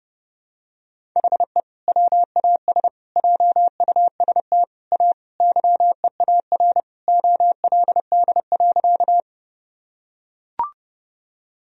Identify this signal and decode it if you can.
Morse code, heard as a keyed tone, about 32 wpm, 700 Hz. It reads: HE WAS JUST A YEAR OLD.
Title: Sentences from Top 1000 Words - Review Part 1 - 30wpm